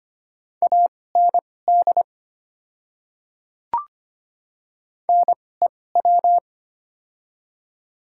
Morse code, heard as a keyed tone, about 25 wpm, spaced as for 18 wpm, 700 Hz. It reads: AND NEW
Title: 2x Character Spacing - Top 200 Words - 25wpm